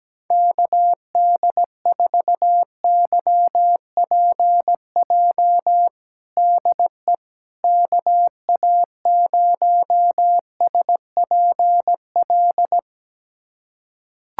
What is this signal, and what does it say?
Morse code, heard as a keyed tone, about 17 wpm, 700 Hz. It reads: KD4YPJ DE KA0SPL
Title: USA Callsign DE Callsign - 6 Characters - 17wpm